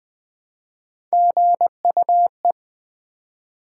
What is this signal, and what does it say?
Morse code, heard as a keyed tone, about 20 wpm, 700 Hz. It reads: GUE
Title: Build A Word - 1 Letter At A Time - 3 to 5-Letter Words - 20wpm